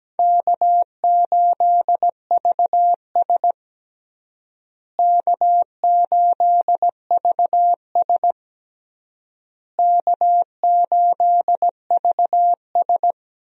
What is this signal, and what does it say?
Morse code, heard as a keyed tone, about 17 wpm, 700 Hz. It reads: K8VS K8VS K8VS
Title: USA Call Signs - 4 Characters - Repeated 3x - 17wpm